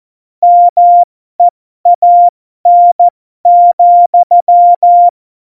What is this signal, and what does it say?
Morse code, heard as a keyed tone, about 14 wpm, spaced as for 12 wpm, 700 Hz. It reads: MEAN,
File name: Morsekurs / Witze